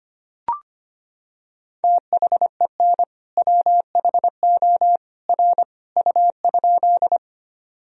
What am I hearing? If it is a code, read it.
Morse code, heard as a keyed tone, about 25 wpm, 700 Hz. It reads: THEN WHO R U?
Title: Sentences from Top 100 Words and Abbreviations 25wpm